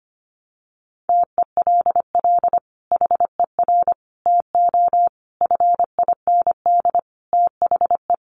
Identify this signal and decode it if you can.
Morse code, heard as a keyed tone, about 25 wpm, 700 Hz. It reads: TELL HER TO FIND THE
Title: Sentences from Top 1200 Words 25wpm